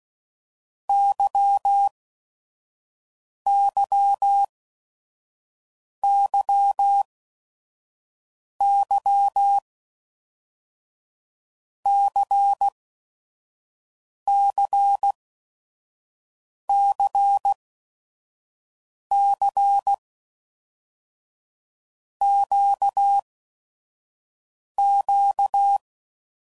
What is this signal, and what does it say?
Morse code, heard as a keyed tone, about 16 wpm, 800 Hz. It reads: Y Y Y Y C C C C Q Q
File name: LFCYQ@5 f15